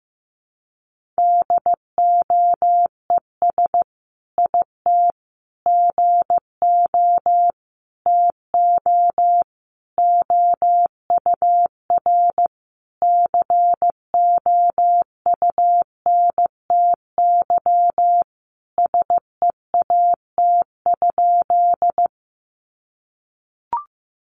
Morse code, heard as a keyed tone, about 15 wpm, 700 Hz. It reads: DOES IT GO TO OUR COUNTY SEAT?
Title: Sentences from Top 1500 Words 15wpm